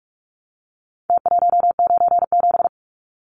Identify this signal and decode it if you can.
Morse code, heard as a keyed tone, about 44 wpm, 700 Hz. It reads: T197
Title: Numbers - Contest Style - 4-Digit 45wpm